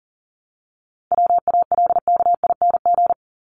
Morse code, heard as a keyed tone, about 40 wpm, 700 Hz. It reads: WALKING
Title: ING Suffix - Top 100 Words with Base Words 40wpm